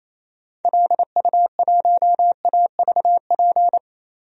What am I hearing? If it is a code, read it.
Morse code, heard as a keyed tone, about 28 wpm, 700 Hz. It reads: LU1AVP
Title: DXCC-100 Call Signs - 6 Characters - Repeated 3x - 28wpm